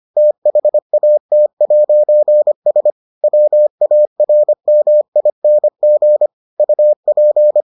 Morse code, heard as a keyed tone, about 25 wpm, 600 Hz. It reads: THAT'S WARMING UP